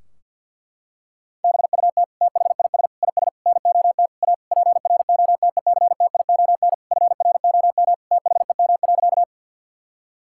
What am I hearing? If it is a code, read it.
Morse code, heard as a keyed tone, about 50 wpm, 700 Hz. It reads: BUT THIS IS NOT A PROTECTION FROM THEM.
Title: Sentences from Top 1600 Words 50wpm